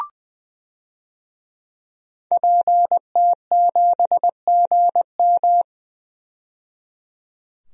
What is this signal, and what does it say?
Morse code, heard as a keyed tone, about 20 wpm, 700 Hz. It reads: PT7GM
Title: DXCC-100 Call Signs - 5 Characters - 20wpm